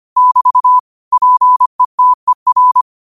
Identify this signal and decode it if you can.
Morse code, heard as a keyed tone, about 25 wpm, 1000 Hz. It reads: <BT> PETER